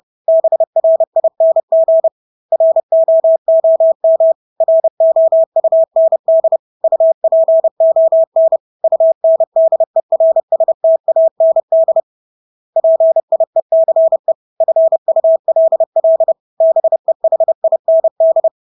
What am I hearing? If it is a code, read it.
Morse code, heard as a keyed tone, about 30 wpm, 650 Hz. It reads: BRING ROOM ROUND UPON UNDERSTAND PIECE FULL BEHIND
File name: Engelske ord 30-30 wpm